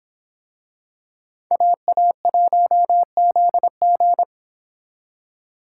Morse code, heard as a keyed tone, about 26 wpm, 700 Hz. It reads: AA1ZG